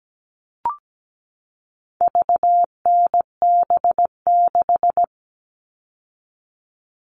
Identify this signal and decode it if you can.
Morse code, heard as a keyed tone, about 17 wpm, 700 Hz. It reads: VNB6
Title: All Cut Numbers - 4-Digit 17wpm